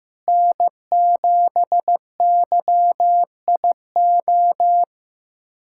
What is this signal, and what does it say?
Morse code, heard as a keyed tone, about 15 wpm, 700 Hz. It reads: N7YIO
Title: USA Callsign DE Callsign - 5 Characters - 15wpm